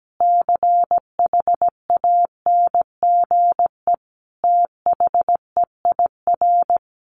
Morse code, heard as a keyed tone, about 17 wpm, 700 Hz. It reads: CHANGE THEIR
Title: Easy Phrases 17wpm